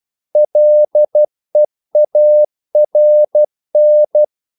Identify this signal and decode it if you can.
Morse code, heard as a keyed tone, about 12 wpm, 600 Hz. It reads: LEARN
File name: CW / Practice / EnglishQuotations0112